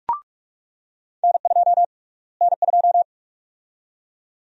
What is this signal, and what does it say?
Morse code, heard as a keyed tone, about 46 wpm, 700 Hz. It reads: N2 N2
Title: USA Call Sign Prefixes - Repeated 2x - 45wpm